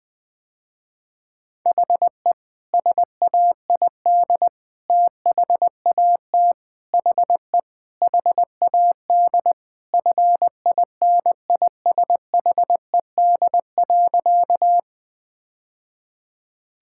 Morse code, heard as a keyed tone, about 21 wpm, 700 Hz. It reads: HE SAID THAT HE HAD FINISHED.
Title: Sentences with 3-Word Phrases - Spoken Beginning - 20wpm